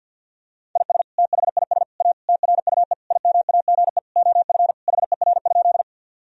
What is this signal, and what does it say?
Morse code, heard as a keyed tone, about 50 wpm, 700 Hz. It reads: IS THIS A TRUE IMAGE OF HER?
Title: Sentences from Top 1000 Words - Review Part 4 - 50wpm